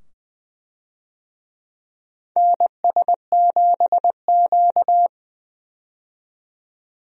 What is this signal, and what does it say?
Morse code, heard as a keyed tone, about 20 wpm, 700 Hz. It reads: NS7Q
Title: USA Call Signs - 4 Characters - 20wpm